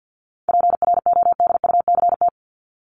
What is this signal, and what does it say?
Morse code, heard as a keyed tone, about 50 wpm, 700 Hz. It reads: PRODUCT
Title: ES:S Suffix - Top 100 Words with Base Words 50wpm